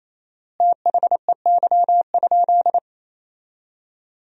Morse code, heard as a keyed tone, about 28 wpm, 700 Hz. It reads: THEY?
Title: 4x Word-Spacing - Sentences from Top 300 Words - 28wpm